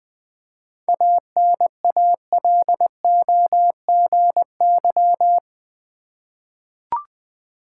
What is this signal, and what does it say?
Morse code, heard as a keyed tone, about 20 wpm, 700 Hz. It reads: ANALOGY
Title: 7 Letter Words - First 3 Letters Given - 20wpm